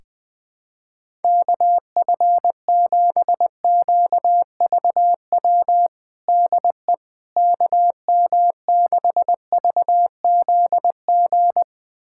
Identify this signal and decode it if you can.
Morse code, heard as a keyed tone, about 20 wpm, 700 Hz. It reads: KF7QVW DE KM6VZG
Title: USA Callsign DE Callsign 20wpm